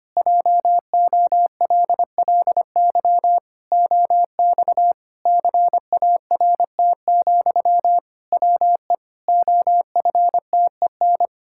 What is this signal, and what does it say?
Morse code, heard as a keyed tone, about 25 wpm, 700 Hz. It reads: JOLLY OX CART, WE OFTEN
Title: Pangram Challenge v2 - 25wpm